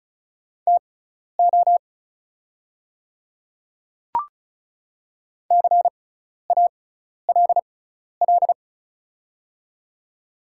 Morse code, heard as a keyed tone, about 36 wpm, spaced as for 12 wpm, 700 Hz. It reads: TO CALL
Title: 6x Character Spacing - Top 100 Words - 35wpm